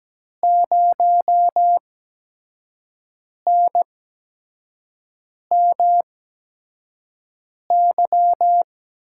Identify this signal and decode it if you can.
Morse code, heard as a keyed tone, about 17 wpm, spaced as for 5 wpm, 700 Hz. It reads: 0NMY
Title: USA Call Signs - 8x Character Spacing - 17wpm